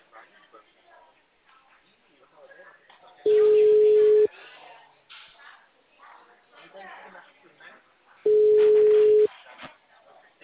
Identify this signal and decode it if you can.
Ringback tone (Europe, 425 Hz), with speech.